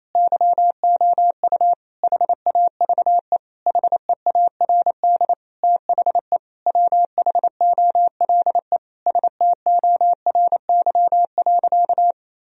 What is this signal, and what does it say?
Morse code, heard as a keyed tone, about 28 wpm, 700 Hz. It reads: YOU HAVE HEARD THE WHOLE STORY.
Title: Sentences from Top 1000 Words - Review Part 3 - 28wpm